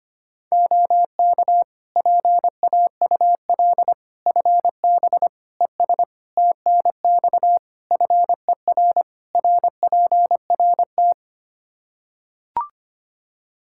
Morse code, heard as a keyed tone, about 25 wpm, 700 Hz. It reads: OK PAUL FB ES TNX FER RPRT
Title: Practice QSO - Exchange 2 - Info - 25wpm